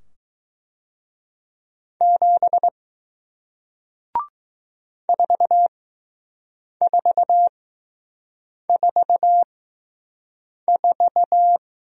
Morse code, heard as a keyed tone, about 20 wpm, spaced as for 6 wpm, 700 Hz. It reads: 7 4444